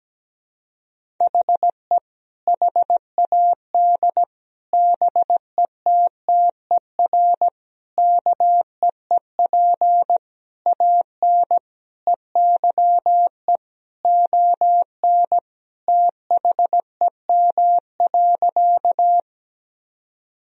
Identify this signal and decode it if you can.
Morse code, heard as a keyed tone, about 17 wpm, 700 Hz. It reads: HE HAD BETTER KEEP AN EYE ON THEM.
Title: Sentences from Top 1000 Words - Review Part 1 - 17wpm